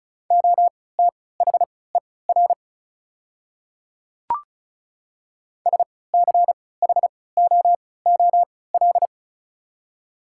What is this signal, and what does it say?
Morse code, heard as a keyed tone, about 34 wpm, spaced as for 20 wpm, 700 Hz. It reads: OTHER SCHOOL
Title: 3x Character Spacing - Top 300 Words - 35wpm